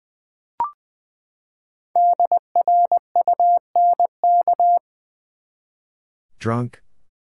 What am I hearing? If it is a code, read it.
Morse code, heard as a keyed tone, about 20 wpm, 700 Hz. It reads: DRUNK
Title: Five Letter Words - 20wpm